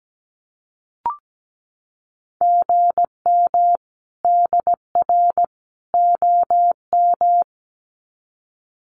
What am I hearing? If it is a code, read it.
Morse code, heard as a keyed tone, about 17 wpm, 700 Hz. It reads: GM DR OM
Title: Common QSO Phrases 17wpm